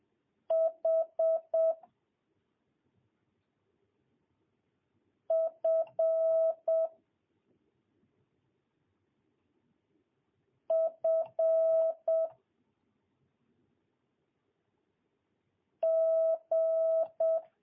Morse code, heard as a keyed tone, about 7 wpm, 650 Hz. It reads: H F F G
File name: jh8cbh / 2022cw